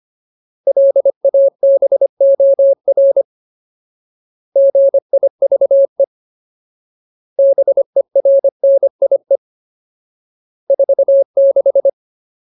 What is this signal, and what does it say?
Morse code, heard as a keyed tone, about 25 wpm, 550 Hz. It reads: LABOR GIVE BERNIE 46
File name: Ultimate Word List / 25WPM Ultimate